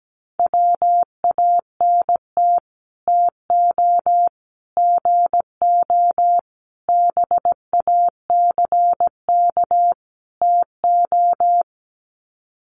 Morse code, heard as a keyed tone, about 17 wpm, 700 Hz. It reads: WANT TO GO BACK TO
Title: Sets of 5 Words 17wpm